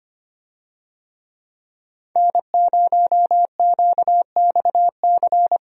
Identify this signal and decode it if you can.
Morse code, heard as a keyed tone, about 25 wpm, 700 Hz. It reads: N0QXC
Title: USA Call Signs - 5 Characters - 25wpm